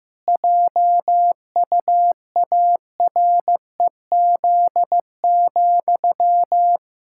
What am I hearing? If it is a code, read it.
Morse code, heard as a keyed tone, about 16 wpm, 700 Hz. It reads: JUAREZ,
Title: Pangram Challenge v2 - 15wpm